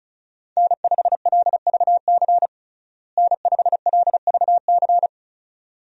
Morse code, heard as a keyed tone, about 34 wpm, 700 Hz. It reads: N5LVC N5LVC
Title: USA Call Signs - 5 Characters - Repeated 2x - 35wpm